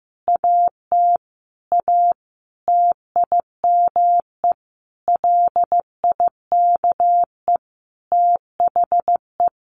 Morse code, heard as a keyed tone, about 15 wpm, 700 Hz. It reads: AT A TIME LIKE THE
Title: Sentences from Top 900 Words 15wpm